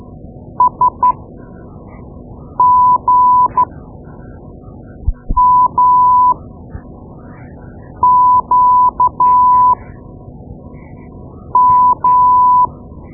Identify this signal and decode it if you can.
Morse code, heard as a keyed tone, about 13 wpm, spaced as for 4 wpm, 1000 Hz. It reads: SGMQM
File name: Morse Lessons / Morse Lessons / Tape-2B